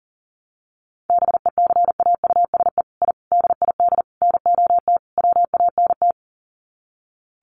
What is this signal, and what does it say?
Morse code, heard as a keyed tone, about 40 wpm, 700 Hz. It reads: BECAUSE I DID NOT WANT
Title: Sets of 5 Words Encore2 40wpm